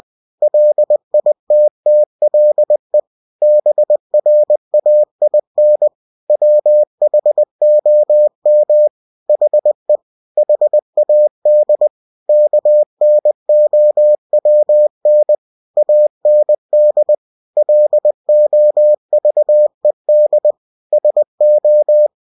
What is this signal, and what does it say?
Morse code, heard as a keyed tone, about 20 wpm, 600 Hz. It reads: LITTLE BRAIN WHOM HE HAD KNOWN AND LOVED SO
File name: cw-books / winnie-the-pooh / winnie-20wpm-page-0100